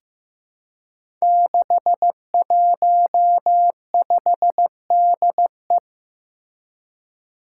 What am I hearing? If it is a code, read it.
Morse code, heard as a keyed tone, about 15 wpm, 700 Hz. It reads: 615DE
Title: Five Challenging Letter-Number Combinations 15wpm